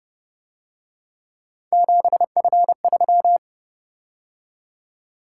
Morse code, heard as a keyed tone, about 30 wpm, 700 Hz. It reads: 7F3
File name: Speed-Racing - Three Letter-Number Combinations - 25wpm